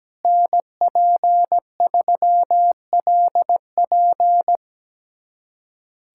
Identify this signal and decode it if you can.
Morse code, heard as a keyed tone, about 17 wpm, 700 Hz. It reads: NP3LP
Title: USA Callsign DE Callsign - 5 Characters - 17wpm